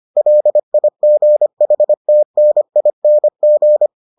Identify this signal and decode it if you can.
Morse code, heard as a keyed tone, about 25 wpm, 600 Hz. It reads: LIGHTNING